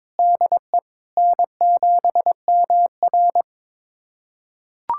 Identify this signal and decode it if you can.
Morse code, heard as a keyed tone, about 22 wpm, 700 Hz. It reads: DE N7MR
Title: USA Callsign DE Callsign - 4 Characters - 22wpm